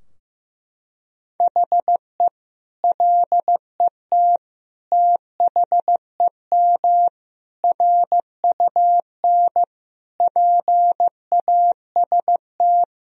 Morse code, heard as a keyed tone, about 15 wpm, 700 Hz. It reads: HE LET THEM RUN PAST
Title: Sentences from Top 1000 Words - Review Part 3 - 15wpm